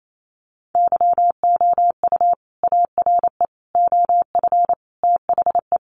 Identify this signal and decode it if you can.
Morse code, heard as a keyed tone, about 28 wpm, 700 Hz. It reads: YOU ARE OF THE
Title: Sentences from Top 1400 Words 28wpm